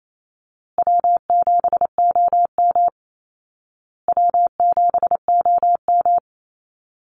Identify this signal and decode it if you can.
Morse code, heard as a keyed tone, about 28 wpm, 700 Hz. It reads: W7OM W7OM
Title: Easy Call Signs Repeated 3x - 28wpm